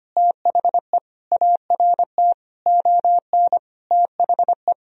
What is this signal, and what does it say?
Morse code, heard as a keyed tone, about 25 wpm, 700 Hz. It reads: THE ART ON THE
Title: Sentences from Top 1000 Words - Review Part 3 - 25wpm